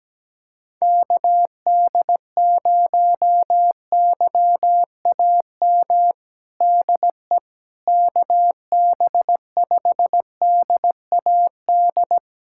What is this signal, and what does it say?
Morse code, heard as a keyed tone, about 17 wpm, 700 Hz. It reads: KD0YAM DE KB5DAD
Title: USA Callsign DE Callsign - 6 Characters - 17wpm